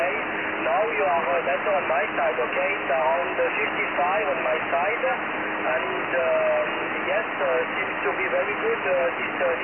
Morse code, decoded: none